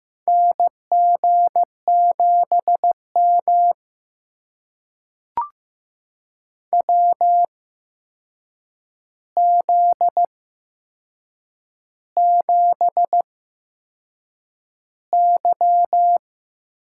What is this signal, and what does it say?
Morse code, heard as a keyed tone, about 15 wpm, spaced as for 6 wpm, 700 Hz. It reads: NG7M WZ7Y
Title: USA Call Signs - 4 Characters - 8x Character Spacing - 15wpm